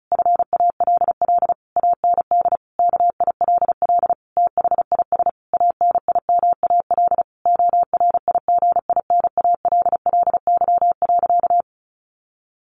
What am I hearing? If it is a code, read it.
Morse code, heard as a keyed tone, about 34 wpm, 700 Hz. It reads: FALL AND KILL THIS ANIMAL ORIGINALLY.
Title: A Sound Of Thunder 35wpm